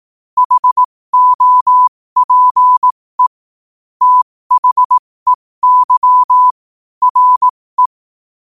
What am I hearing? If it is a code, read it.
Morse code, heard as a keyed tone, about 18 wpm, spaced as for 15 wpm, 1000 Hz. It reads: HOPE THEY RE
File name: qotdfiles / 15wpm